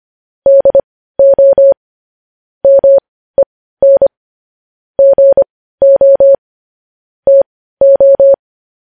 Morse code, heard as a keyed tone, about 25 wpm, spaced as for 15 wpm, 550 Hz. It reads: DO MEN GO TO
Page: news headlines in morse code at 15 wpm